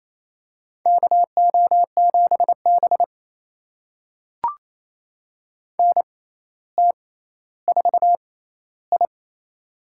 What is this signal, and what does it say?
Morse code, heard as a keyed tone, about 28 wpm, spaced as for 14 wpm, 700 Hz. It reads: KO7B NT4I
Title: USA Call Signs - 4 Characters - 6x Character Spacing - 28wpm